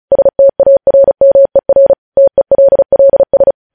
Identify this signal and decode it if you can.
Morse code, heard as a keyed tone, about 36 wpm, 550 Hz. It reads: STARMER TELLS
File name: podcast / News headlines-1x-35x35